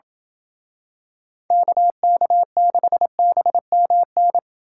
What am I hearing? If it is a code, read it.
Morse code, heard as a keyed tone, about 28 wpm, 700 Hz. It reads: KK6BMN